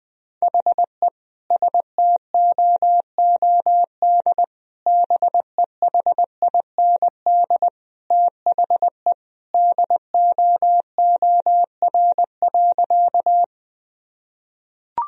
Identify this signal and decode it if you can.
Morse code, heard as a keyed tone, about 20 wpm, 700 Hz. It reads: HE STOOD BEHIND THE DOOR.
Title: Sentences from Top 1000 Words - Review Part 2 - 20wpm